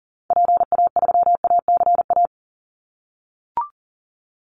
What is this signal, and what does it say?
Morse code, heard as a keyed tone, about 40 wpm, 700 Hz. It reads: PA3ACA
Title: Call Signs - 40wpm